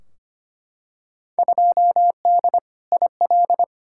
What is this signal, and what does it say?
Morse code, heard as a keyed tone, about 25 wpm, 700 Hz. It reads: 2D IL
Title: Field Day Exchanges - 25wpm